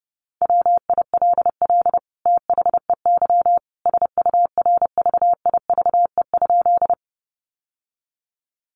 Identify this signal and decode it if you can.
Morse code, heard as a keyed tone, about 30 wpm, 700 Hz. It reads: WILL THEY SURVIVE?
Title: Sentences from Top 1300 Words 30wpm